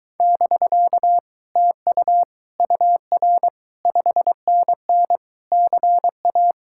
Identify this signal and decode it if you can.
Morse code, heard as a keyed tone, about 23 wpm, 700 Hz. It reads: <BK> TU UR 5NN CA